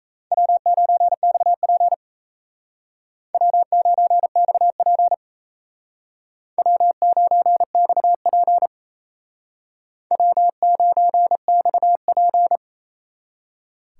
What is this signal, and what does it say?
Morse code, heard as a keyed tone, about 35 wpm, 700 Hz. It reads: W9XP W9XP W9XP W9XP